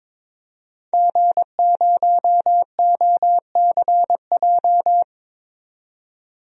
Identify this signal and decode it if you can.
Morse code, heard as a keyed tone, about 22 wpm, 700 Hz. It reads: G0OCJ
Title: DXCC-100 Call Signs - 5 Characters - 22wpm